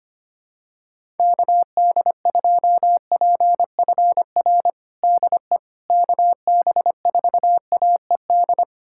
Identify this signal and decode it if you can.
Morse code, heard as a keyed tone, about 25 wpm, 700 Hz. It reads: KD2PFR DE KB4AED
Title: USA Callsign DE Callsign - 6 Characters - 25wpm